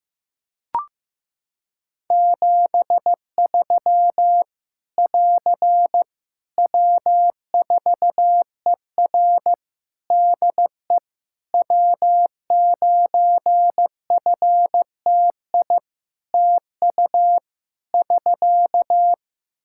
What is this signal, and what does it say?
Morse code, heard as a keyed tone, about 15 wpm, 700 Hz. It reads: 73 <AR> W4ER DE W9FTI TU <SK>
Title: Practice QSO - Exchange 4 - Ending - 15wpm